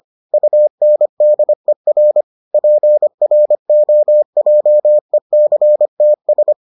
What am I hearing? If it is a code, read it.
Morse code, heard as a keyed tone, about 25 wpm, 600 Hz. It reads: UNDER PROJECTS